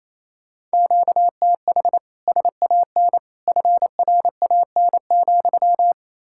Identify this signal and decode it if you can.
Morse code, heard as a keyed tone, about 35 wpm, 700 Hz. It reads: QTH SAN FRAN,
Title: Speed-Racing Common QSO Phrases Encore - 28wpm